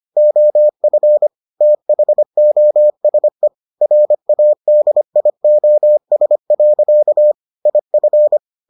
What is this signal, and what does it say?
Morse code, heard as a keyed tone, about 25 wpm, 600 Hz. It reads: OF THOSE RADIOS. IF